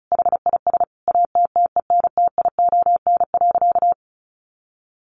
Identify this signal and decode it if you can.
Morse code, heard as a keyed tone, about 34 wpm, 700 Hz. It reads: HIS ATTENTION.
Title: Sentences from Top 1400 Words 35wpm